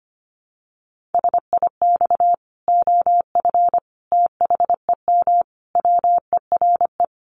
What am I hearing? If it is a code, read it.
Morse code, heard as a keyed tone, about 25 wpm, 700 Hz. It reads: SIX OF THEM WERE
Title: Sentences from Top 1100 Words 25wpm